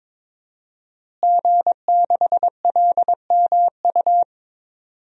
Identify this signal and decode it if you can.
Morse code, heard as a keyed tone, about 22 wpm, 700 Hz. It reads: G6LMU
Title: DXCC-100 Call Signs - 5 Characters - Repeated 3x - 22wpm